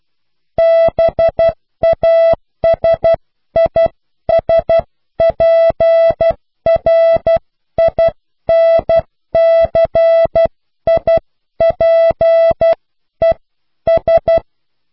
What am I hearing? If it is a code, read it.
Morse code, heard as a keyed tone, about 12 wpm, spaced as for 11 wpm, 660 Hz. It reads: BASISPRINCIPES